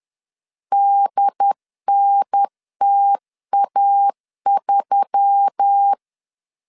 Morse code, heard as a keyed tone, about 10 wpm, 800 Hz. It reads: DNTA3
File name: LEZIONI / 7 GF9Z / 7 G 10wpm 2501